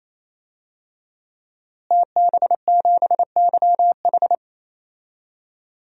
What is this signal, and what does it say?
Morse code, heard as a keyed tone, about 28 wpm, 700 Hz. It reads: TB7YH
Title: DXCC-100 Call Signs - 5 Characters - Repeated 3x - 28wpm